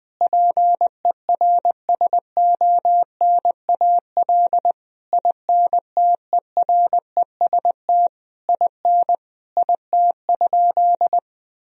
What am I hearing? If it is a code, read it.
Morse code, heard as a keyed tone, about 19 wpm, 700 Hz. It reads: PERSONAL INTEREST IN IT?
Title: Sentences from Top 1000 Words - Review Part 4 - 20wpm